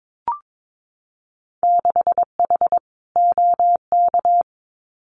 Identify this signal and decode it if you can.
Morse code, heard as a keyed tone, about 22 wpm, 700 Hz. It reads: 6H OK
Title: Winter Field Day Exchanges Encore 22wpm